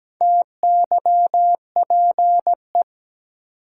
Morse code, heard as a keyed tone, about 17 wpm, 700 Hz. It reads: TYPE
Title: Sentences from Top 600 Words - New Word Drill - 17wpm